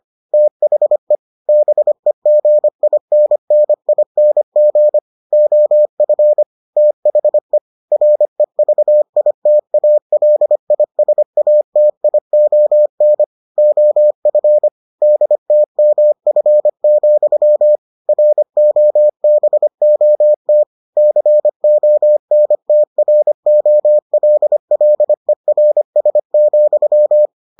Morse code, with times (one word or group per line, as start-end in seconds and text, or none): 0.3-1.1 THE
1.5-5.0 BEGINNING
5.3-6.4 OF
6.8-7.6 THE
7.9-13.2 REVITALISATION
13.6-14.7 OF
15.0-17.8 DTMF,
18.1-20.6 ROBOT
21.0-27.3 CONTROLLERS,